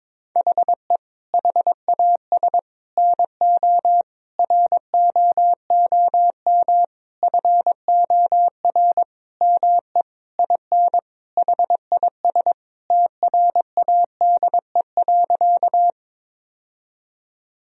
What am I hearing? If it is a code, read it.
Morse code, heard as a keyed tone, about 22 wpm, 700 Hz. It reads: HE HAS NO ROOM FOR ME IN HIS TRADE.
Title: Sentences from Top 1000 Words - Review Part 4 - 22wpm